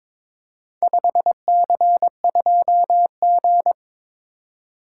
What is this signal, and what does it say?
Morse code, heard as a keyed tone, about 22 wpm, 700 Hz. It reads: 5C2G
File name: practice / 22wpm / Four Letter-Number Combinations - 22wpm